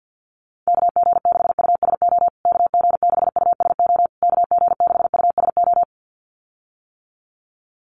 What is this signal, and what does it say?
Morse code, heard as a keyed tone, about 48 wpm, 700 Hz. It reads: KG6USO KG6USO KG6USO
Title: USA Call Signs Repeated 3x 50wpm